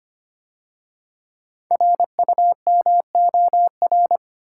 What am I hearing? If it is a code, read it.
Morse code, heard as a keyed tone, about 25 wpm, 700 Hz. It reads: RUMOR
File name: 5 Letter Words - First 3 Letters Given - 25wpm